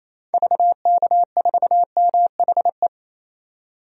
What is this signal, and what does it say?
Morse code, heard as a keyed tone, about 28 wpm, 700 Hz. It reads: VK4MHE
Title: DXCC-100 Call Signs - 6 Characters - 28wpm